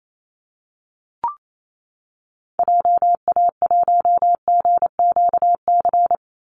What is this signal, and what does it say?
Morse code, heard as a keyed tone, about 28 wpm, 700 Hz. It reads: JA1GQC
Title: Call Signs - 28wpm